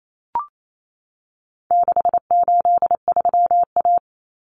Morse code, heard as a keyed tone, about 28 wpm, 700 Hz. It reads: 683A
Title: Common Cut Numbers - 4-Digit 28wpm